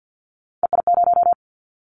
Morse code, heard as a keyed tone, about 50 wpm, 700 Hz. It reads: EI0
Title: Callsign Prefixes for Top 50 DXCC - 50wpm